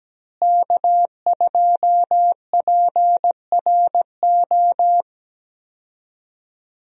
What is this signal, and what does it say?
Morse code, heard as a keyed tone, about 17 wpm, 700 Hz. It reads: K2PRO
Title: USA Call Signs - 5 Characters - Repeated 3x - 17wpm